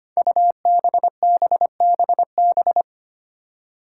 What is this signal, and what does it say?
Morse code, heard as a keyed tone, about 25 wpm, 700 Hz. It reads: UBBBB
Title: Five Challenging Letter-Number Combinations 25wpm